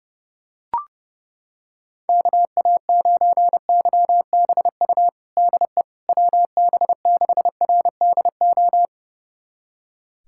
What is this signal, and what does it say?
Morse code, heard as a keyed tone, about 30 wpm, 700 Hz. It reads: KA9YBU DE WB6RDO
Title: USA Callsign DE Callsign 30wpm